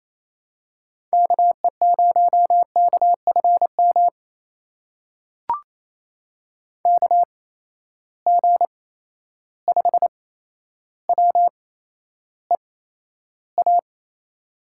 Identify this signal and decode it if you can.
Morse code, heard as a keyed tone, about 28 wpm, spaced as for 12 wpm, 700 Hz. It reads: KE0KFM KG5WEA
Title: USA Call Signs - 6 Characters - 8x Character Spacing - 28wpm